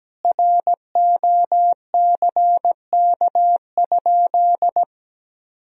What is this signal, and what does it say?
Morse code, heard as a keyed tone, about 17 wpm, 700 Hz. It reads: ROCK?
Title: Sentences from Top 1000 Words - Review Part 2 - 17wpm